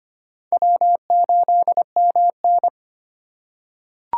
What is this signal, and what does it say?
Morse code, heard as a keyed tone, about 25 wpm, 700 Hz. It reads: W8MN
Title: USA Call Signs - 4 Characters - 6x Character Spacing - 25wpm